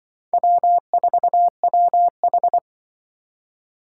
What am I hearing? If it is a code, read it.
Morse code, heard as a keyed tone, about 24 wpm, 700 Hz. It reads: W4WH
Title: Speed Racing - USA 1x2 Call Signs 20wpm